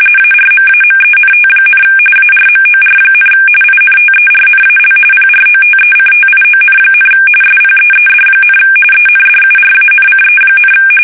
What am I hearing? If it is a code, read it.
RTTY, about 45.45 baud. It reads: ARMAMENT, ARMOR, INTERNAL PROTECTION, SPEED, ENDURANCE,
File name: fsk850 / battleshipnj